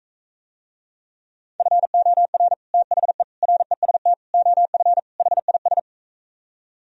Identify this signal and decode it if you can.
Morse code, heard as a keyed tone, about 44 wpm, 700 Hz. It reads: FOR THE REST OF HIS